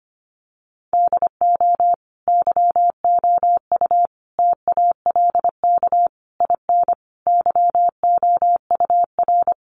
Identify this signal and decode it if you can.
Morse code, heard as a keyed tone, about 25 wpm, 700 Hz. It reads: DO YOU TALK IN YOUR
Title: Sentences from Top 500 Words 25wpm